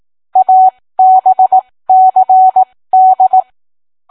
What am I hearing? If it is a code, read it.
Morse code, heard as a keyed tone, about 18 wpm, spaced as for 15 wpm, 750 Hz. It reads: ABCD